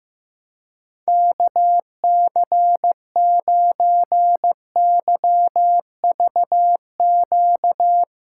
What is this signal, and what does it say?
Morse code, heard as a keyed tone, about 16 wpm, 700 Hz. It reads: KC9YVQ
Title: USA Callsign DE Callsign - 6 Characters - 15wpm